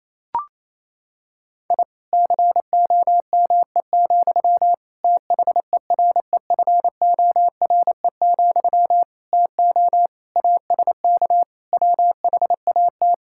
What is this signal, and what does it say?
Morse code, heard as a keyed tone, about 28 wpm, 700 Hz. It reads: I COME, THEREFORE, TO ASK WHAT
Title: Sentences from Top 1000 Words - Review Part 5 - 28wpm